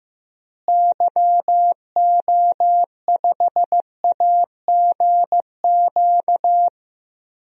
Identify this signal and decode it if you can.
Morse code, heard as a keyed tone, about 15 wpm, 700 Hz. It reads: YO5AGQ
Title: DXCC-100 Call Signs - 6 Characters - Repeated 2x - 15wpm